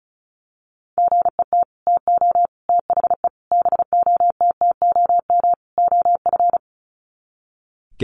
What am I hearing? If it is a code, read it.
Morse code, heard as a keyed tone, about 34 wpm, 700 Hz. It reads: GET TO THE BOTTOM OF
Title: Sets of 5 Words Encore 35wpm